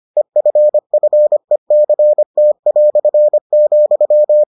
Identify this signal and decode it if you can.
Morse code, heard as a keyed tone, about 25 wpm, 600 Hz. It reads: EFFECT",